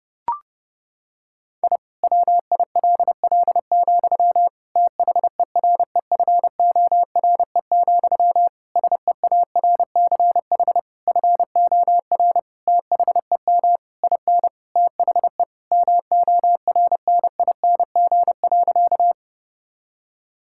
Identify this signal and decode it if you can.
Morse code, heard as a keyed tone, about 32 wpm, 700 Hz. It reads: I WILL, THEREFORE, SEARCH FOR THEM IN THE MORNING.
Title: Sentences from Top 1000 Words - Review Part 5 - 30wpm